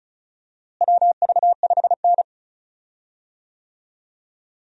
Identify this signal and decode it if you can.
Morse code, heard as a keyed tone, about 36 wpm, 700 Hz. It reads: WV5N
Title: USA Call Signs - 4 Characters - Repeated 3x - 35wpm